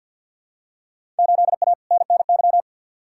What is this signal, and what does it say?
Morse code, heard as a keyed tone, about 52 wpm, 700 Hz. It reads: 8A NNY
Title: Field Day Exchanges Encore - 50wpm